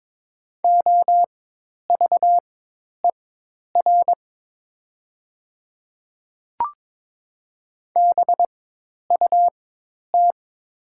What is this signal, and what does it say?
Morse code, heard as a keyed tone, about 22 wpm, spaced as for 10 wpm, 700 Hz. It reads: OVER BUT
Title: 4x Character Spacing - Top 100 Words - 22wpm